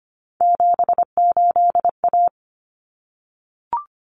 Morse code, heard as a keyed tone, about 25 wpm, 700 Hz. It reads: 78A
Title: Common Cut Numbers - 3-Digit 25wpm